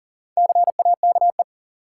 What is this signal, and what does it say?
Morse code, heard as a keyed tone, about 37 wpm, 700 Hz. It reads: CAKE
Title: Four Letter Words - 40wpm